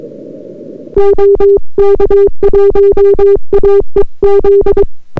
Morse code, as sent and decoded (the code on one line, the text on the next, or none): --- -.- .---- .- . --..
OK1AEZ